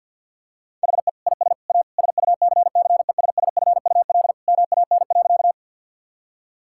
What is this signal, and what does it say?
Morse code, heard as a keyed tone, about 52 wpm, 700 Hz. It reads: HE IS A SUCCESSFUL MAN.
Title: Sentences from Top 1000 Words - Review Part 6 - 50wpm